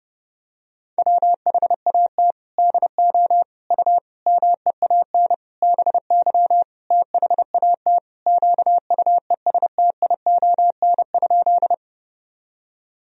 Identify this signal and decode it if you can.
Morse code, heard as a keyed tone, about 30 wpm, 700 Hz. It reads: WHAT DO U MEAN BY THAT QUESTION?
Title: Sentences from Top 400 Words and Abbreviations 30wpm